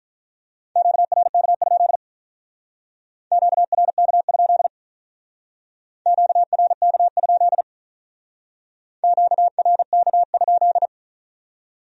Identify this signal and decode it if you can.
Morse code, heard as a keyed tone, about 46 wpm, 700 Hz. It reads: QRK? QRK? QRK? QRK?